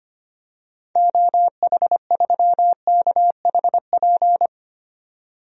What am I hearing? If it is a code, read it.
Morse code, heard as a keyed tone, about 25 wpm, 700 Hz. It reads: OH3KHP
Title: DXCC-100 Call Signs - 6 Characters - 25wpm